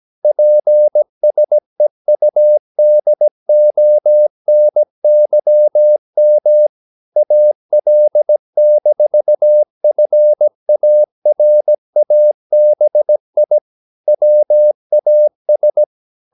Morse code, heard as a keyed tone, about 17 wpm, 600 Hz. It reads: PSEUDONYM AL-FARABI WAS
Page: Romanen Never af Ken Follet på engelsk.